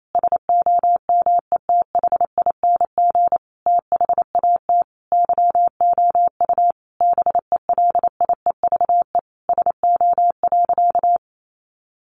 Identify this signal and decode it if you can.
Morse code, heard as a keyed tone, about 28 wpm, 700 Hz. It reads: SOMETHING THAT YOU BELIEVE SO.
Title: Sentences from Top 1000 Words 28wpm